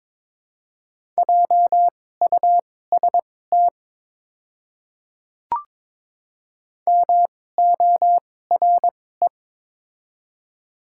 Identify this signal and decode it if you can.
Morse code, heard as a keyed tone, about 22 wpm, spaced as for 16 wpm, 700 Hz. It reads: JUST MORE
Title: 2x Character Spacing - Top 200 Words - 22wpm